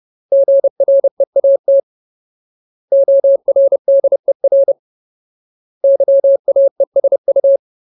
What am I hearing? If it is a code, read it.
Morse code, heard as a keyed tone, about 30 wpm, 550 Hz. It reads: GREAT ORDER YAESU
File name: Ultimate Word List / 30WPM Ultimate